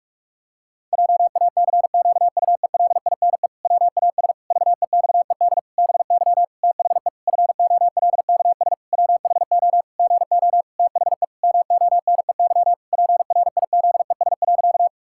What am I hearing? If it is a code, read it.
Morse code, heard as a keyed tone, about 46 wpm, 700 Hz. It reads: JACQUELINE WAS VEXED BY THE FOLKS WHO GO THE MONEY PRIZES.
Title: Pangram Challenge v2 - 45wpm